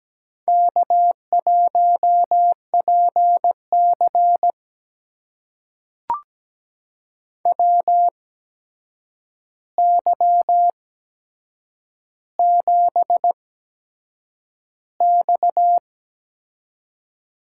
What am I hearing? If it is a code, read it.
Morse code, heard as a keyed tone, about 17 wpm, spaced as for 7 wpm, 700 Hz. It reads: K1PC WY7X
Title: USA Call Signs - 4 Characters - 8x Character Spacing - 17wpm